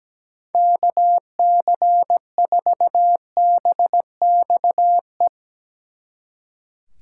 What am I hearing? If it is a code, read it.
Morse code, heard as a keyed tone, about 17 wpm, 700 Hz. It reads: KC4BXE
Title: Speed Racing - USA 2x3 Call Signs 17wpm